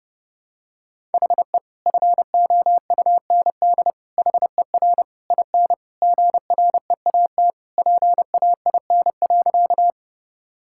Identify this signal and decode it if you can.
Morse code, heard as a keyed tone, about 32 wpm, 700 Hz. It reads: HE FOUND HER IN GREAT PAIN.
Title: Sentences from Top 1600 Words 30wpm